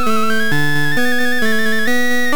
There is no code.